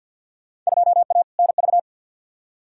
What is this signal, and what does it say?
Morse code, heard as a keyed tone, about 48 wpm, 700 Hz. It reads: 2A NV
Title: Field Day Exchanges - 50wpm